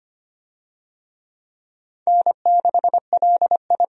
Morse code, heard as a keyed tone, about 25 wpm, 700 Hz. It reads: N6LI
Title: USA Call Signs - 4 Characters - 25wpm